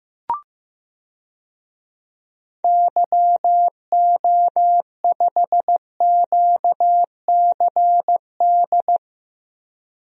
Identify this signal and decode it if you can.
Morse code, heard as a keyed tone, about 15 wpm, 700 Hz. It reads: YO5QCD
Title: DXCC-100 Call Signs - 6 Characters - Repeated 2x - 15wpm